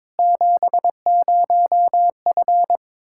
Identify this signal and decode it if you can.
Morse code, heard as a keyed tone, about 22 wpm, 700 Hz. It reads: 70F
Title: K1USN SST QSO Practice 22wpm